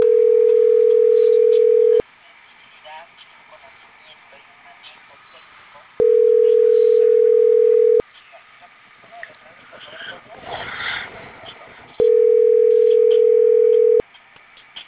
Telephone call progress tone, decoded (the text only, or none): ringback tone